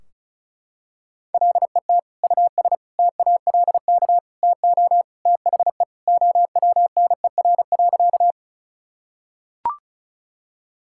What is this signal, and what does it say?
Morse code, heard as a keyed tone, about 36 wpm, 700 Hz. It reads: LET US TALK TO THE OWNER.
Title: Sentences from Top 1600 Words 35wpm